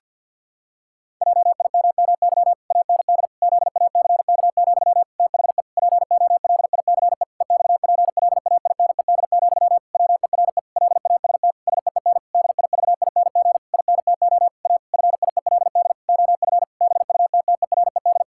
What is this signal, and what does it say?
Morse code, heard as a keyed tone, about 52 wpm, 700 Hz. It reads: JIMMY AND ZACK, THE POLICE EXPLAINED, WERE LAST SEEN DIVING INTO A FIELD OF BUTTERED
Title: Pangram Challenge v2 - 50wpm